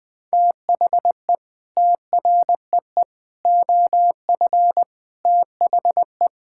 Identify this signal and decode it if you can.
Morse code, heard as a keyed tone, about 21 wpm, 700 Hz. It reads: THE TREE OF THE
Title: Sentences with 9-Word Phrases - Spoken Beginning - 20wpm